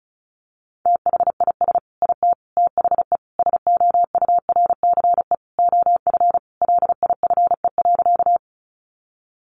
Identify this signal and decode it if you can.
Morse code, heard as a keyed tone, about 34 wpm, 700 Hz. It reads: THIS IT THE SOURCE OF LIFE.
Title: Sentences from Top 800 Words 35wpm